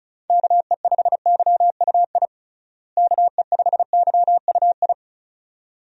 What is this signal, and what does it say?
Morse code, heard as a keyed tone, about 36 wpm, 700 Hz. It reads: KE5YUI KE5YUI
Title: USA Call Signs - 6 Characters - Repeated 2x - 35wpm